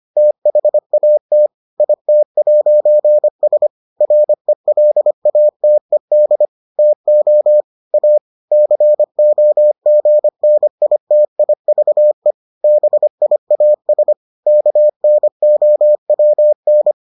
Morse code, heard as a keyed tone, about 25 wpm, 600 Hz. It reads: THAT IT'S RELATED TO A COGNITIVE BIAS KNOWN